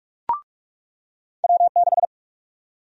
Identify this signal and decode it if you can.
Morse code, heard as a keyed tone, about 47 wpm, 700 Hz. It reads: W6
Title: USA Call Sign Prefixes - 45wpm